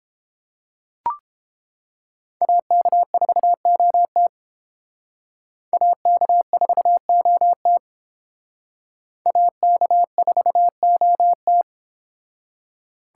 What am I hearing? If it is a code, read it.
Morse code, heard as a keyed tone, about 30 wpm, 700 Hz. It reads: AK4OT AK4OT AK4OT